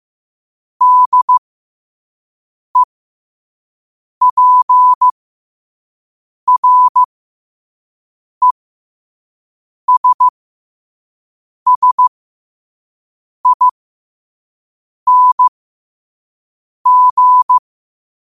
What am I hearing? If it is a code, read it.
Morse code, heard as a keyed tone, about 15 wpm, spaced as for 5 wpm, 1000 Hz. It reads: DEPRESSING
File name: qotdfiles / 5wpm